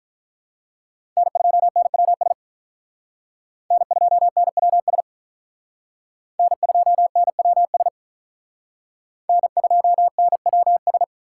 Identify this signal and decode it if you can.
Morse code, heard as a keyed tone, about 45 wpm, 700 Hz. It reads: N2NWS N2NWS N2NWS N2NWS